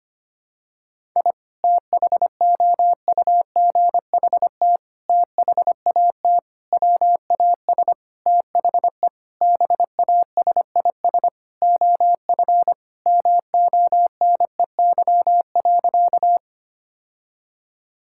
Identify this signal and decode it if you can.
Morse code, heard as a keyed tone, about 25 wpm, 700 Hz. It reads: I THOUGHT THAT WAS THE BASIS OF MONEY.
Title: Sentences from Top 1000 Words - Review Part 5 - 25wpm